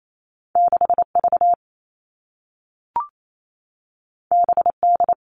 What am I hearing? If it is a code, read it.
Morse code, heard as a keyed tone, about 28 wpm, 700 Hz. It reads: 6V BD
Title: All Cut Numbers - 2-Digit 28wpm